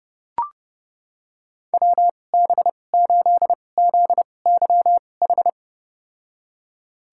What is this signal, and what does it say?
Morse code, heard as a keyed tone, about 30 wpm, spaced as for 22 wpm, 700 Hz. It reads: WB8ZYH
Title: USA Call Signs - 6 Characters - 2x Character Spacing - 30wpm